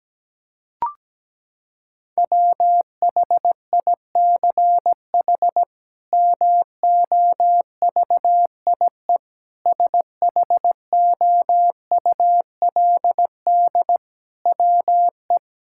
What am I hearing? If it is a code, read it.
Morse code, heard as a keyed tone, about 17 wpm, 700 Hz. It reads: WHICH MOVIE SHOULD WE
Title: Sentences from Top 1000 Words - Review Part 5 - 17wpm